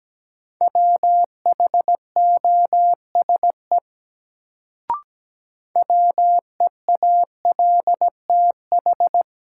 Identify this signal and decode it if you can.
Morse code, heard as a keyed tone, about 17 wpm, 700 Hz. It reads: WHOSE WEALTH
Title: Sentences from Top 1200 Words - New Word Drill Alt - 17wpm